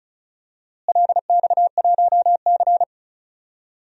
Morse code, heard as a keyed tone, about 36 wpm, 700 Hz. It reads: LX1C
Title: DXCC-100 Call Signs - 4 Characters - Repeated 2x - 35wpm